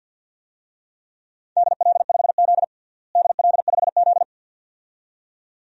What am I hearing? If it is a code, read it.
Morse code, heard as a keyed tone, about 46 wpm, 700 Hz. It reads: DL5Z DL5Z
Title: DXCC-100 Call Signs - 4 Characters - Repeated 2x - 50wpm